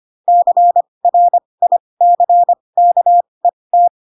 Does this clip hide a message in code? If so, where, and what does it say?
Yes: Morse code from 0.3 to 3.9 s, heard as a keyed tone, about 25 wpm, spaced as for 20 wpm, 700 Hz. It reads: CRICKET